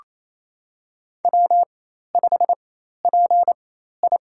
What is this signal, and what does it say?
Morse code, heard as a keyed tone, about 28 wpm, spaced as for 13 wpm, 700 Hz. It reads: W5PI
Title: USA Call Signs - 4 Characters - 4x Character Spacing - 28wpm